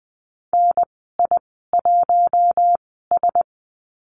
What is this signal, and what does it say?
Morse code, heard as a keyed tone, about 20 wpm, spaced as for 15 wpm, 700 Hz. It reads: NI1S
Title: 055 Lesson 10g - Letter r - Call Signs 20wpm